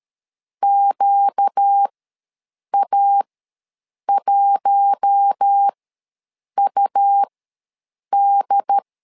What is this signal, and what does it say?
Morse code, heard as a keyed tone, about 13 wpm, spaced as for 7 wpm, 800 Hz. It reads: QA1UD
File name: LEZIONI / 4 Q1S3 ws 12wpm 2038